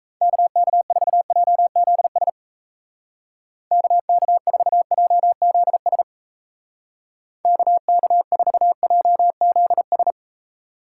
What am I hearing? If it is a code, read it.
Morse code, heard as a keyed tone, about 37 wpm, 700 Hz. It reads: KK4JZS KK4JZS KK4JZS